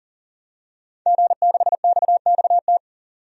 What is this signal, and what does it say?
Morse code, heard as a keyed tone, about 40 wpm, 700 Hz. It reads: G6XXT
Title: DXCC-100 Call Signs - 5 Characters - 40wpm